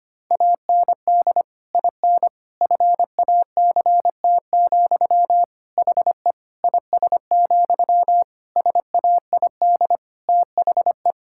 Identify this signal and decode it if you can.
Morse code, heard as a keyed tone, about 25 wpm, 700 Hz. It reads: AND IN FACT, HE IS, SAID THE
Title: Sentences from Top 1600 Words 25wpm